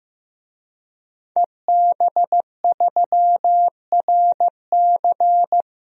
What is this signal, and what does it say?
Morse code, heard as a keyed tone, about 15 wpm, 700 Hz. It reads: EB3RC
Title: DXCC-100 Call Signs - 5 Characters - 15wpm